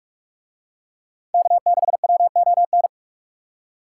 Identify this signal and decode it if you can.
Morse code, heard as a keyed tone, about 42 wpm, 700 Hz. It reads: K6WON